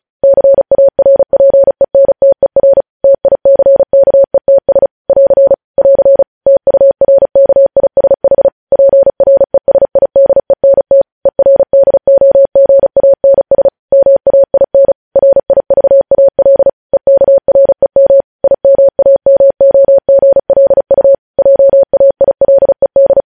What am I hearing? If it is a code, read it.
Morse code, heard as a keyed tone, about 36 wpm, 550 Hz. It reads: CARPENTER TICKETS <AR> <AR> TURKISH PRESIDENT ERDOGANS MAIN RIVAL EKREM IMAMOGLU JAILED
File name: podcast / News headlines-1x-35x35